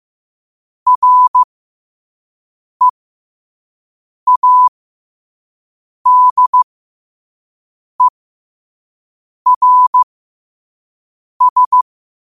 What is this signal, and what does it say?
Morse code, heard as a keyed tone, about 15 wpm, spaced as for 5 wpm, 1000 Hz. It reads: READERS
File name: qotdfiles / 5wpm